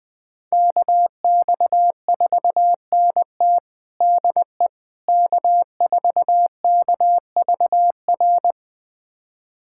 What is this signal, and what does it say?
Morse code, heard as a keyed tone, about 20 wpm, 700 Hz. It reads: KX4NT DE K4KVR
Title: USA Callsign DE Callsign - 5 Characters - 20wpm